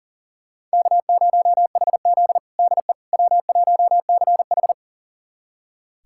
Morse code, heard as a keyed tone, about 40 wpm, 700 Hz. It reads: K0HZ DE W1CH
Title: USA Callsign DE Callsign - 4 Characters - 40wpm